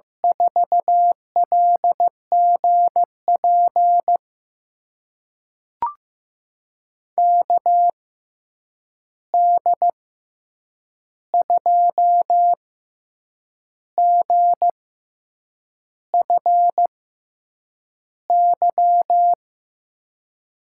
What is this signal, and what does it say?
Morse code, heard as a keyed tone, about 15 wpm, spaced as for 7 wpm, 700 Hz. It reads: 4LGP KD2GFY
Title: USA Call Signs - 6 Characters - 6x Character Spacing - 15wpm